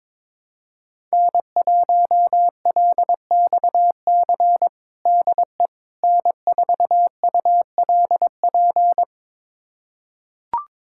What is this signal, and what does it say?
Morse code, heard as a keyed tone, about 22 wpm, 700 Hz. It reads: N1LXC DE N4ULP
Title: USA Callsign DE Callsign - 5 Characters - 22wpm